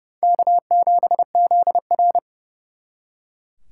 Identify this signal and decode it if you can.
Morse code, heard as a keyed tone, about 36 wpm, 700 Hz. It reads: K7ZR
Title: Speed Racing - USA 1x2 Call Signs 30wpm